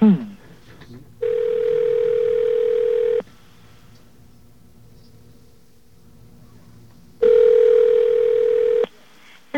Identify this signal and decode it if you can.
Ringback tone (North America), 440 + 480 Hz.